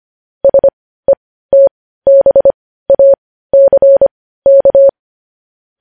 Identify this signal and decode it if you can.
Morse code, heard as a keyed tone, about 25 wpm, spaced as for 15 wpm, 550 Hz. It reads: SETBACK
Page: news-headlines-in-morse-code-at-15-wpm